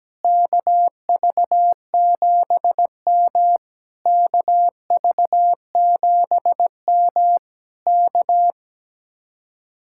Morse code, heard as a keyed tone, about 17 wpm, 700 Hz. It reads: KV7M KV7M K
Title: Speed-Racing Common QSO Phrases - 17wpm